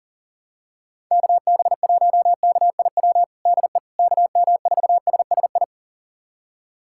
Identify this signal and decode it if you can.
Morse code, heard as a keyed tone, about 37 wpm, 700 Hz. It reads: KB1KIW DE KM4SSI
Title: USA Callsign DE Callsign 40wpm